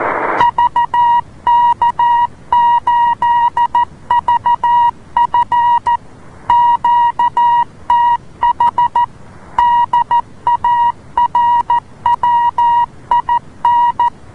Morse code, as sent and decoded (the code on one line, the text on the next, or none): ...- -.- ---.. ...- ..-. / --.- - .... / -.. .- .-. .-- .. -.
VK8VF QTH DARWIN